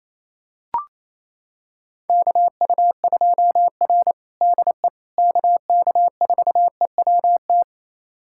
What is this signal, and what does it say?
Morse code, heard as a keyed tone, about 28 wpm, 700 Hz. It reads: KU2R DE KK4EWT
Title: USA Callsign DE Callsign 28wpm